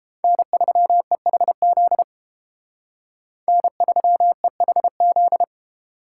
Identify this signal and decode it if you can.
Morse code, heard as a keyed tone, about 29 wpm, 700 Hz. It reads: N3EHZ N3EHZ